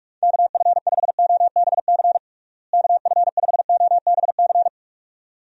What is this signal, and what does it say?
Morse code, heard as a keyed tone, about 46 wpm, 700 Hz. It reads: KF5OBC KF5OBC
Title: USA Call Signs - 6 Characters - Repeated 2x - 45wpm